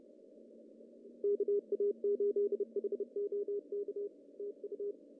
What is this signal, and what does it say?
Morse code, heard as a keyed tone, about 30 wpm, 410 Hz. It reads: KA8HOK TU
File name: CW / cqww-cw-k5tr